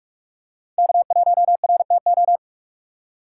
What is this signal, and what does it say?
Morse code, heard as a keyed tone, about 46 wpm, 700 Hz. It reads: K1RTO
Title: USA Call Signs - 5 Characters - 6x Character Spacing - 45wpm